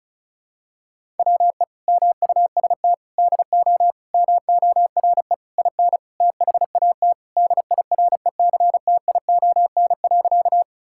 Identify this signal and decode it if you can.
Morse code, heard as a keyed tone, about 36 wpm, 700 Hz. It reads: WE MUST DO MORE IN THAT DIRECTION.
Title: Sentences from Top 1000 Words - Review Part 5 - 35wpm